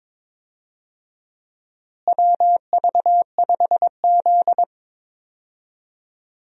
Morse code, heard as a keyed tone, about 22 wpm, 700 Hz. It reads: WV5Z
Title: USA Call Signs - 4 Characters - Repeated 3x - 22wpm